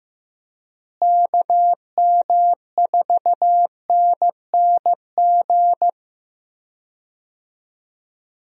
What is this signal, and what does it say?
Morse code, heard as a keyed tone, about 15 wpm, 700 Hz. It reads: KM4NNG
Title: USA Call Signs - 6 Characters - Repeated 2x - 15wpm